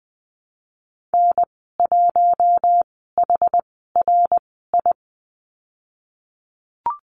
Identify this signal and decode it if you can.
Morse code, heard as a keyed tone, about 20 wpm, 700 Hz. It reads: N 1 H R I
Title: 062 Lesson 11g - Letter h - Call Signs 20wpm